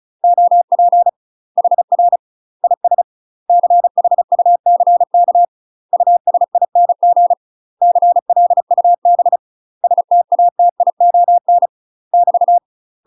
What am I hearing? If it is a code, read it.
Morse code, heard as a keyed tone, about 35 wpm, 700 Hz. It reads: OP HR IS CHUCK USING CLUB STATION <BT>
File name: UK MASTER QSOs TRAINING/35WPM